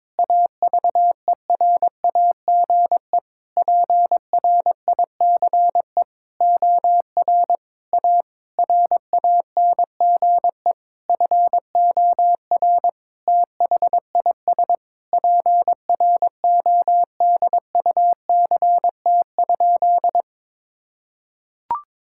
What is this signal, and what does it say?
Morse code, heard as a keyed tone, about 22 wpm, 700 Hz. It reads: AVERAGE PRICE OR A RANGE FOR THIS PRODUCT?
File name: Sentences with 9-Word Phrases - 22wpm